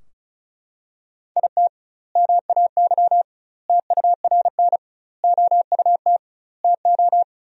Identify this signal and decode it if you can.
Morse code, heard as a keyed tone, about 34 wpm, 700 Hz. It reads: IT MAY TURN OUT TO
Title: 2x Word-Spacing - Sentences from Top 400 Words - 35wpm